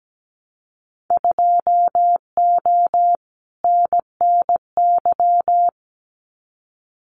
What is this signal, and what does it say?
Morse code, heard as a keyed tone, about 17 wpm, 700 Hz. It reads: 2O NNY
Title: Winter Field Day Exchanges 17wpm